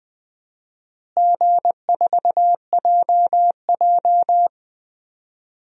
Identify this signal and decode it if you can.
Morse code, heard as a keyed tone, about 21 wpm, 700 Hz. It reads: G4JJ
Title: DXCC-100 Call Signs - 4 Characters - Repeated 3x - 20wpm